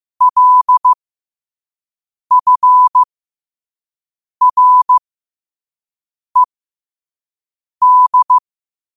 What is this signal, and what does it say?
Morse code, heard as a keyed tone, about 15 wpm, spaced as for 5 wpm, 1000 Hz. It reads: LFRED
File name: qotdfiles / 5wpm